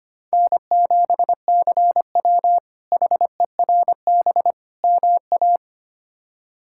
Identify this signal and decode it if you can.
Morse code, heard as a keyed tone, about 25 wpm, 700 Hz. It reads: N7CW HERB MA
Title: K1USN SST QSO Practice - 25wpm